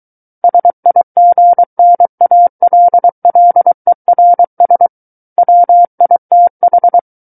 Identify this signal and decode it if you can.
Morse code, heard as a keyed tone, about 23 wpm, 700 Hz. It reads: SIGNALLERS WITH
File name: Book One / 14-Chapter 14 - 23 WPM